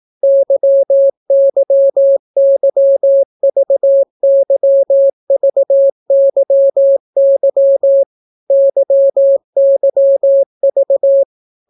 Morse code, heard as a keyed tone, about 18 wpm, 550 Hz. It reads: YYYVYVYY YYV